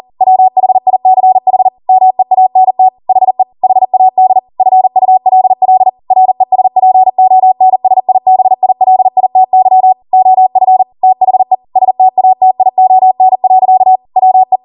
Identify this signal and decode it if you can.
Morse code, heard as a keyed tone, about 40 wpm, 750 Hz. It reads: WHICH MEANT HE HAD FULL RESPONSIBILITY OF THE STATION. WE